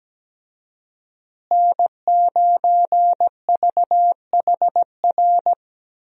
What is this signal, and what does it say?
Morse code, heard as a keyed tone, about 17 wpm, 700 Hz. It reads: N9VHR
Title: USA Call Signs - 5 Characters - Repeated 3x - 17wpm